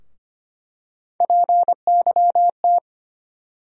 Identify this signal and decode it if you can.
Morse code, heard as a keyed tone, about 25 wpm, 700 Hz. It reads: PYT
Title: USA Call Sign Postfixes - 3 Letters 25wpm